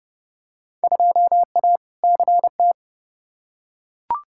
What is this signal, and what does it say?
Morse code, heard as a keyed tone, about 29 wpm, 700 Hz. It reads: 2A CT
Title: Field Day Exchanges - 30wpm